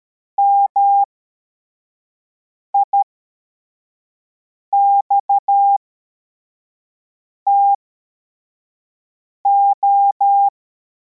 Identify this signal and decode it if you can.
Morse code, heard as a keyed tone, about 13 wpm, spaced as for 4 wpm, 800 Hz. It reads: MIXTO
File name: formation / cw / Text61